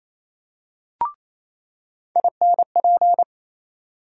Morse code, heard as a keyed tone, about 28 wpm, 700 Hz. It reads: INP
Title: USA Call Sign Postfixes - 3 Letters 28wpm